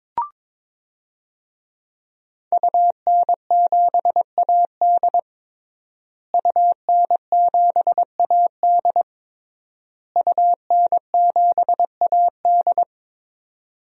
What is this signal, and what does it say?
Morse code, heard as a keyed tone, about 22 wpm, 700 Hz. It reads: UN7AD UN7AD UN7AD
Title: DXCC-100 Call Signs - 5 Characters - Repeated 3x - 22wpm